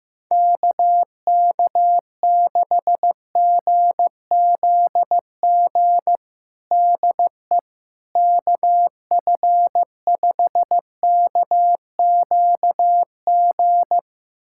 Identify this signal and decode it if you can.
Morse code, heard as a keyed tone, about 15 wpm, 700 Hz. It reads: KK6GZG DE KF5KQG
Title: USA Callsign DE Callsign - 6 Characters - 15wpm